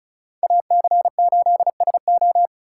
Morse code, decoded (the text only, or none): AC8SO